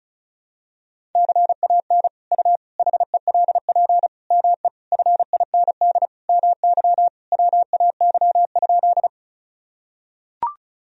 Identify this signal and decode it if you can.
Morse code, heard as a keyed tone, about 36 wpm, 700 Hz. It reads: CAN U HELP ME FIND MY WAY?
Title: Sentences from Top 200 Words and Abbreviations 35wpm